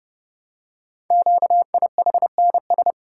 Morse code, heard as a keyed tone, about 30 wpm, 700 Hz. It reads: QIHNS
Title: Code Groups - 5 Letters - 30wpm